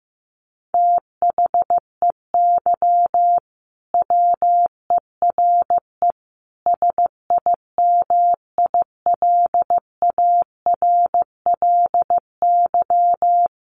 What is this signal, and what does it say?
Morse code, heard as a keyed tone, about 15 wpm, 700 Hz. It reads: THEY WERE SIMILARLY
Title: Sentences from Top 1400 Words 15wpm